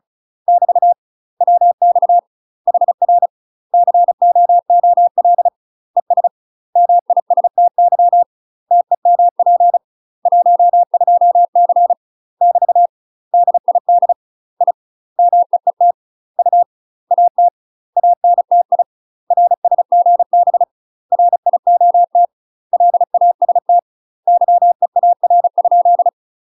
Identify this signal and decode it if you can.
Morse code, heard as a keyed tone, about 36 wpm, 700 Hz. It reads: <BT> WX HR COOL ES MISTY TEMP 12C <BT> DID I MEET U AT ANTI RSGB RIOT LAST YEAR?
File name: UK MASTER QSOs TRAINING / 35WPM / UK MASTER QSOs 0014